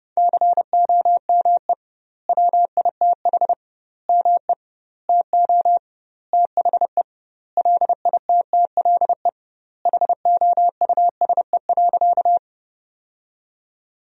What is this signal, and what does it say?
Morse code, heard as a keyed tone, about 30 wpm, 700 Hz. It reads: COME WITH ME TO THE LITTLE HOUSE.
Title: 2x Word-Spacing - Sentences from Top 200 Words - 30wpm